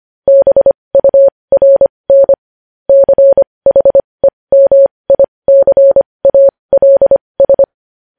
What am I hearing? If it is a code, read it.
Morse code, heard as a keyed tone, about 25 wpm, spaced as for 20 wpm, 550 Hz. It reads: BURN CHEMICALS
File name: podcast / News headlines-1x-25x20